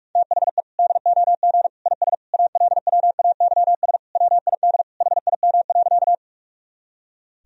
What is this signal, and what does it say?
Morse code, heard as a keyed tone, about 46 wpm, 700 Hz. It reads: THE DOG IS ALWAYS WID HIM.
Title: Sentences from Top 400 Words and Abbreviations 45wpm